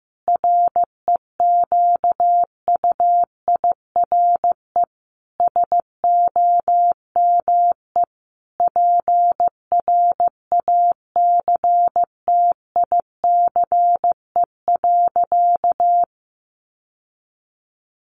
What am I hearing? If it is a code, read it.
Morse code, heard as a keyed tone, about 15 wpm, 700 Hz. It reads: REQUIRE SOME PRACTICE.
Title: Sentences from Top 900 Words 15wpm